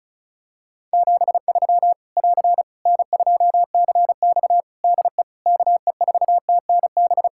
Morse code, heard as a keyed tone, about 34 wpm, 700 Hz. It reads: 73 <AR> N2CX DE KE4TNB
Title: Practice QSO - Exchange 4 - Ending - 35wpm